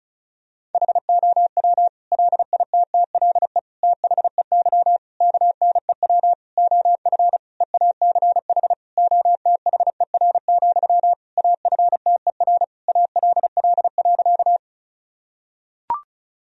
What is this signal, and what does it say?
Morse code, heard as a keyed tone, about 36 wpm, 700 Hz. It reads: HOW LITTLE THEY KNEW OF EACH OTHER, AFTER ALL.
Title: Sentences from Top 1000 Words - Review Part 1 - 35wpm